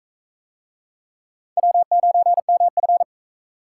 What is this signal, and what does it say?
Morse code, heard as a keyed tone, about 40 wpm, 700 Hz. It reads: W9MF